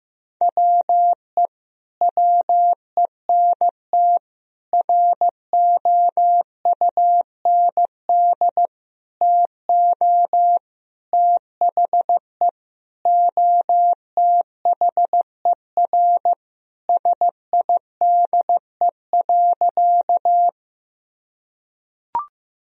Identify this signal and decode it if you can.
Morse code, heard as a keyed tone, about 16 wpm, 700 Hz. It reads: WE WENT ROUND TO THE OTHER SIDE.
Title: Sentences from Top 1000 Words - Review Part 1 - 15wpm